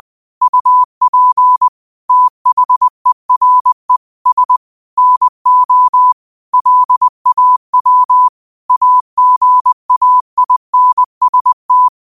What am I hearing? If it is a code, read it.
Morse code, heard as a keyed tone, about 20 wpm, 1000 Hz. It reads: UP THERE S NO LAW AGAINST
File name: qotdfiles / 20wpm